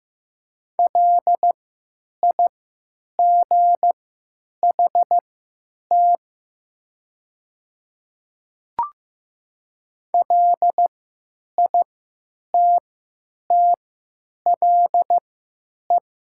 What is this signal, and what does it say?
Morse code, heard as a keyed tone, about 15 wpm, spaced as for 8 wpm, 700 Hz. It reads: LIGHT LITTLE
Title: 3x Character Spacing - Top 200 Words - 15wpm